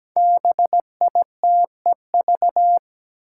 Morse code, heard as a keyed tone, about 17 wpm, 700 Hz. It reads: BITEV
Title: Five Challenging Letter-Number Combinations 17wpm